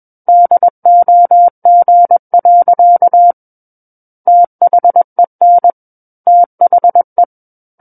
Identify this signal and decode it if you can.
Morse code, heard as a keyed tone, about 21 wpm, 700 Hz. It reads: DOG. THEN THE
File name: Book One / 12-Chapter 12 - 21 WPM